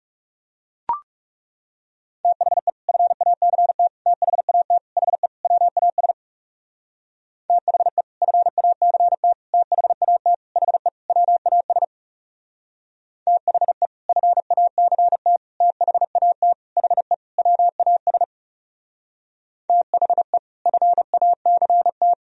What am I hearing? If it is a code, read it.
Morse code, heard as a keyed tone, about 37 wpm, 700 Hz. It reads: THE FACT THAT HE WAS THE FACT THAT HE WAS THE FACT THAT HE WAS THE FACT
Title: Speed-Racing - Sets of 5 Words 30wpm